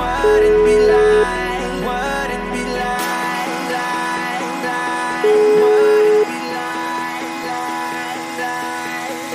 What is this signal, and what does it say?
Ringback tone (Europe, 425 Hz), with noise.